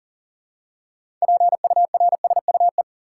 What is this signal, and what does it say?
Morse code, heard as a keyed tone, about 40 wpm, 700 Hz. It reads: PURSUE
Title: Six Letter Words - 40wpm